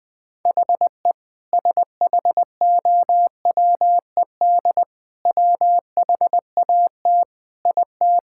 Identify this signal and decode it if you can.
Morse code, heard as a keyed tone, about 20 wpm, 700 Hz. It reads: HE SHOWED WHAT IT
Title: Sentences with 5-Word Phrases - 20wpm